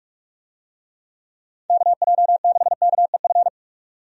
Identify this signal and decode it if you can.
Morse code, heard as a keyed tone, about 46 wpm, 700 Hz. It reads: KJ6KEF
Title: USA Call Signs - 6 Characters - 45wpm